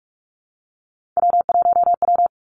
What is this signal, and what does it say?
Morse code, heard as a keyed tone, about 46 wpm, 700 Hz. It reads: W1W
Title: Three Challenging Letter-Number Combinations 45wpm